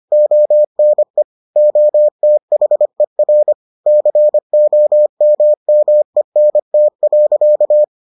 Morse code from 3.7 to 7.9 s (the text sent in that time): COMMENT.